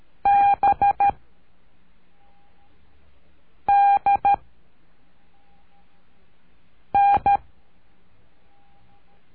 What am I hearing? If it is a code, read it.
Morse code, heard as a keyed tone, about 13 wpm, 800 Hz. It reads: B D N
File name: cursos / CURSO DE TELEGRAFIA / Telegrafia - Trilha 03